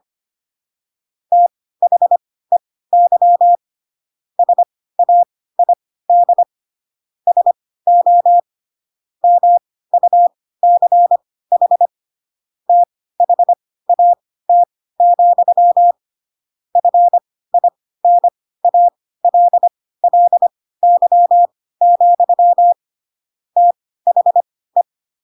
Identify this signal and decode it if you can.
Morse code, heard as a keyed tone, about 25 wpm, spaced as for 16 wpm, 700 Hz. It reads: THEY SAID SO MUCH THAT, FINALLY, THE
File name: Pinocchio / Pinocchio ch3 10 25 ik1hge